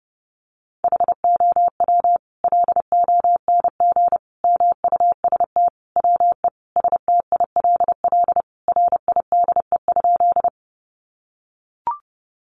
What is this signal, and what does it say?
Morse code, heard as a keyed tone, about 30 wpm, 700 Hz. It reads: HOW LONG MUST WE STILL RIDE?
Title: Sentences from Top 1200 Words 30wpm